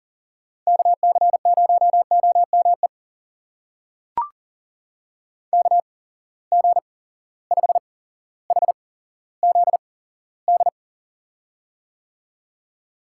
Morse code, heard as a keyed tone, about 39 wpm, spaced as for 17 wpm, 700 Hz. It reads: KC0OME KG5HZD
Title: USA Call Signs - 6 Characters - 8x Character Spacing - 40wpm